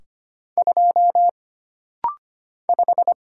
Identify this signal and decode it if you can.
Morse code, heard as a keyed tone, about 25 wpm, 700 Hz. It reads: 2 5
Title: Single Letter-Number Alt - Rapid-Fire 25wpm